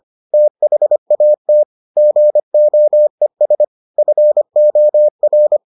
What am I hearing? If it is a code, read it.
Morse code, heard as a keyed tone, about 25 wpm, 600 Hz. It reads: THAT GOES FOR